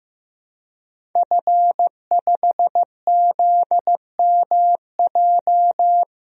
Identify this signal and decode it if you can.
Morse code, heard as a keyed tone, about 15 wpm, 700 Hz. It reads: F5ZMJ
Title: DXCC-100 Call Signs - 5 Characters - Repeated 2x - 15wpm